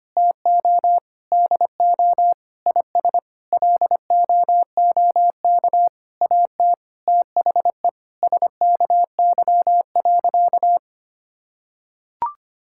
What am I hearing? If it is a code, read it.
Morse code, heard as a keyed tone, about 25 wpm, 700 Hz. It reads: TO DO IS LOOK AT THE SKY.
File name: practice / 25wpm / Sentences with 9-Word Phrases - 25wpm